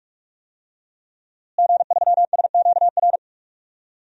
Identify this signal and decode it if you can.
Morse code, heard as a keyed tone, about 43 wpm, 700 Hz. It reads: G3SQR
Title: DXCC-100 Call Signs - 5 Characters - Repeated 2x - 45wpm